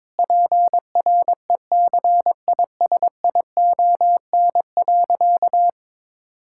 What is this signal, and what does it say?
Morse code, heard as a keyed tone, about 22 wpm, 700 Hz. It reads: PRECISION.
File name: Difficult Sentences with Hints - 22wpm